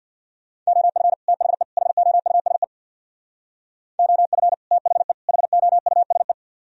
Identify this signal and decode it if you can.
Morse code, heard as a keyed tone, about 66 wpm, 700 Hz. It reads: OF THE HOUSE OF THE HOUSE